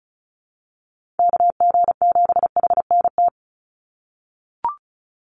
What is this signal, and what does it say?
Morse code, heard as a keyed tone, about 34 wpm, 700 Hz. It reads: KG7HNT
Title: USA Call Signs 35wpm